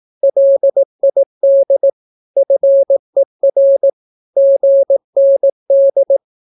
Morse code, heard as a keyed tone, about 18 wpm, 550 Hz. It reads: LID FER GND
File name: naukapodstaw / skroty / skroty